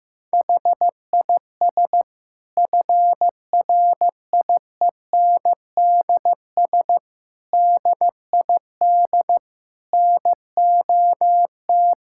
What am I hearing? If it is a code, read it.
Morse code, heard as a keyed tone, about 16 wpm, 700 Hz. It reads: HIS FRIENDS DID NOT
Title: Sentences from Top 1600 Words 15wpm